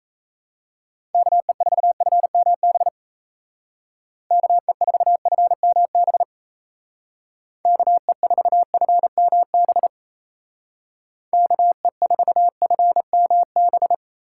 Morse code, heard as a keyed tone, about 36 wpm, 700 Hz. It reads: KE4FMB KE4FMB KE4FMB KE4FMB